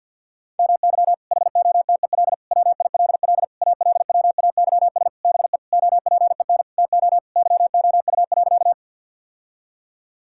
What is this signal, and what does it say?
Morse code, heard as a keyed tone, about 50 wpm, 700 Hz. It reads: MY HOTEL WILL ALWAYS BE OPEN TO YOU.
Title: Sentences from Top 1600 Words 50wpm